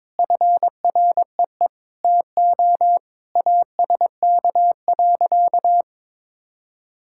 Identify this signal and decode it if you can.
Morse code, heard as a keyed tone, about 21 wpm, 700 Hz. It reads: FREE TO ASK.
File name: practice / 22wpm / Sentences with 8-Word Phrases - 22wpm